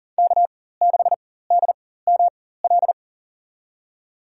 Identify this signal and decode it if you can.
Morse code, heard as a keyed tone, about 40 wpm, spaced as for 19 wpm, 700 Hz. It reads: K6DML
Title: USA Call Signs - 5 Characters - 4x Character Spacing - 40wpm